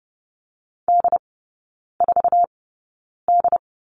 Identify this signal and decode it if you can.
Morse code, heard as a keyed tone, about 30 wpm, spaced as for 9 wpm, 700 Hz. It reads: D4D
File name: 081 Lesson 14e - Number 2 - Three Characters 30wpm